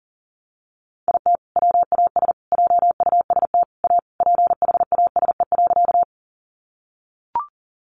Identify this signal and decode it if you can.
Morse code, heard as a keyed tone, about 40 wpm, 700 Hz. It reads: IT WAS JUST A PHASE.
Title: Sentences from Top 1200 Words 40wpm